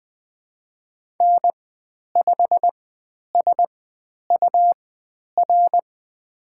Morse code, heard as a keyed tone, about 20 wpm, spaced as for 10 wpm, 700 Hz. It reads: N5SUR
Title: 109 Lesson 18g - Letter m - Call Signs 20wpm